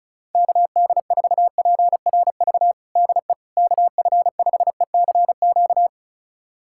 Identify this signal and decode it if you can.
Morse code, heard as a keyed tone, about 36 wpm, 700 Hz. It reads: KD4PRV DE KF5ECQ
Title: USA Callsign DE Callsign - 6 Characters - 35wpm